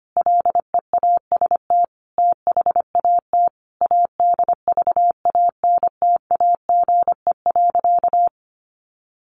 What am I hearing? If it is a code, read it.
Morse code, heard as a keyed tone, about 25 wpm, 700 Hz. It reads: LEAST THAT ADVANTAGE.
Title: Sentences from Top 900 Words 25wpm